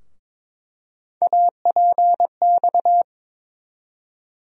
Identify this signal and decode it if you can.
Morse code, heard as a keyed tone, about 21 wpm, 700 Hz. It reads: APX
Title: USA Call Sign Postfixes - 3 Letters 22wpm